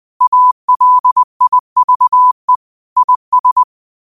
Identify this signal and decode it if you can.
Morse code, heard as a keyed tone, about 20 wpm, 1000 Hz. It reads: ALIVE IS
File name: qotdfiles / 20wpm